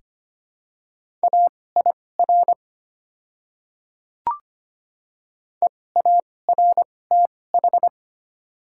Morse code, heard as a keyed tone, about 25 wpm, spaced as for 18 wpm, 700 Hz. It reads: AIR EARTH
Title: 2x Character Spacing - Top 300 Words - 25wpm